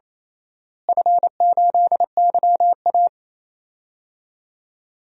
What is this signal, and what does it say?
Morse code, heard as a keyed tone, about 28 wpm, 700 Hz. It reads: F8YA
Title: DXCC-100 Call Signs - 4 Characters - Repeated 2x - 28wpm